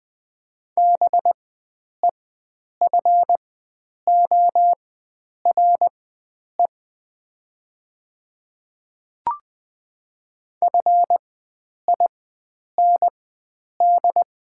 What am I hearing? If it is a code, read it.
Morse code, heard as a keyed tone, about 20 wpm, spaced as for 9 wpm, 700 Hz. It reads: BEFORE FIND
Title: 4x Character Spacing - Top 500 Words - 20wpm